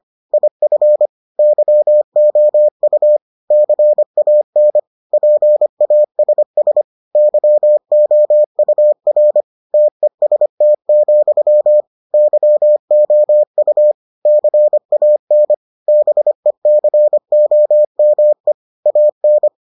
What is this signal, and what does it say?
Morse code, heard as a keyed tone, about 25 wpm, 600 Hz. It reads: IF YOU CAN PASS YOUR TEST, YOU CAN BECOME AN